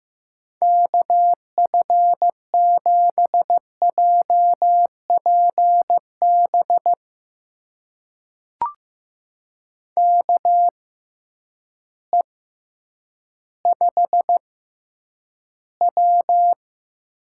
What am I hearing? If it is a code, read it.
Morse code, heard as a keyed tone, about 15 wpm, spaced as for 9 wpm, 700 Hz. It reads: KF7JPB KE5W
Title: USA Call Signs - 6 Characters - 6x Character Spacing - 15wpm